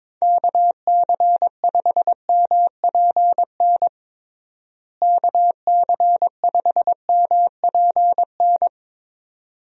Morse code, heard as a keyed tone, about 22 wpm, 700 Hz. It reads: KC5MPN KC5MPN
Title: USA Call Signs - 6 Characters - Repeated 2x - 22wpm